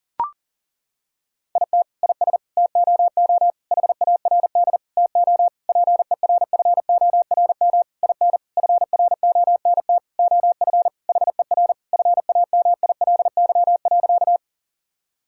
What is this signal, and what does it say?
Morse code, heard as a keyed tone, about 40 wpm, 700 Hz. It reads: IT IS TOO HARD TO PERFORM IN FRONT OF HER FAMILY.
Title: Sentences from Top 1000 Words - Review Part 5 - 40wpm